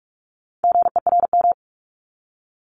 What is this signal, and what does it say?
Morse code, heard as a keyed tone, about 46 wpm, 700 Hz. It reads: GERM
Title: Multi-letter Phonemes 45wpm